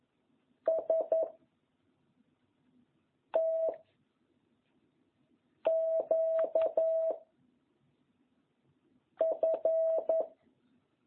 Morse code, heard as a keyed tone, about 11 wpm, 660 Hz. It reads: S T Q F